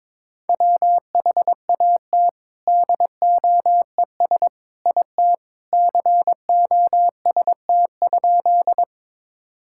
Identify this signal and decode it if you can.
Morse code, heard as a keyed tone, about 22 wpm, 700 Hz. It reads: WHAT DOES IT COST?
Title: Sentences from Top 1000 Words - Review Part 5 - 22wpm